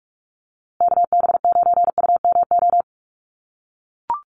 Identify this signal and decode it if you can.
Morse code, heard as a keyed tone, about 44 wpm, 700 Hz. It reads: KB9UMO
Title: USA Call Signs Repeated 2x 45wpm